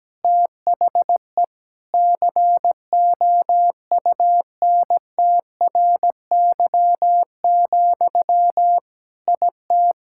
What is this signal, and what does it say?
Morse code, heard as a keyed tone, about 17 wpm, 700 Hz. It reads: THE COUNTRY, IT
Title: Sentences from Top 1000 Words - Review Part 5 - 17wpm